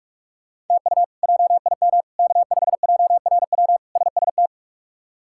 Morse code, heard as a keyed tone, about 46 wpm, 700 Hz. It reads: TU JIM K5JRW SST
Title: K1USN SST QSO Practice - 45wpm